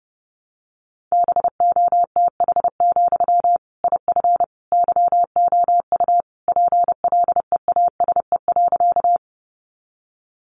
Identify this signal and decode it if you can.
Morse code, heard as a keyed tone, about 30 wpm, 700 Hz. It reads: BOTH, IF YOU PLEASE.
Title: Sentences from Top 1500 Words 30wpm